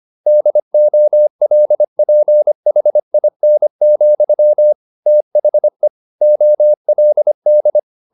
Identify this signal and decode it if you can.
Morse code, heard as a keyed tone, about 25 wpm, 600 Hz. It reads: DOLPHIN, THE OLD
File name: the old man and the sea part 1 25wpm